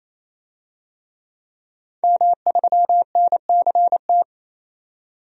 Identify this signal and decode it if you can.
Morse code, heard as a keyed tone, about 28 wpm, 700 Hz. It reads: M3NCT
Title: DXCC-100 Call Signs - 5 Characters - 28wpm